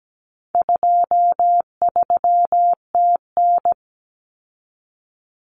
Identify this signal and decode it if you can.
Morse code, heard as a keyed tone, about 17 wpm, 700 Hz. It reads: 23TN
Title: Common Cut Numbers - 4-Digit 17wpm